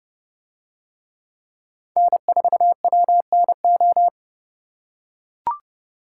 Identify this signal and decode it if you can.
Morse code, heard as a keyed tone, about 29 wpm, 700 Hz. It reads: N4WNO